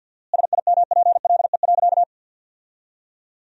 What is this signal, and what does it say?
Morse code, heard as a keyed tone, about 47 wpm, 700 Hz. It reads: SIMPLE.
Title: 4x Word-Spacing - Sentences from Top 500 Words - 50wpm